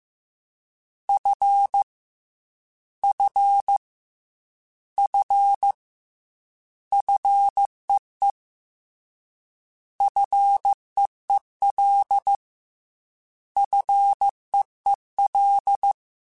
Morse code, heard as a keyed tone, about 15 wpm, 800 Hz. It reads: F F F FEE FEEL FEEL
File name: Short Runs / Short Runs-mp3s / SinglePrefixes @13